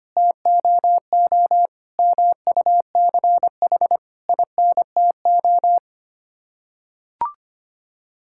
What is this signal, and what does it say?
Morse code, heard as a keyed tone, about 25 wpm, 700 Hz. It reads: TOO MUCH INTO
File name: Sets of 9 Words - 25wpm